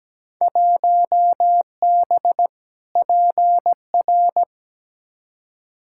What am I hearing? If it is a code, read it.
Morse code, heard as a keyed tone, about 17 wpm, 700 Hz. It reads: 1B PR
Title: Field Day Exchanges Encore - 17wpm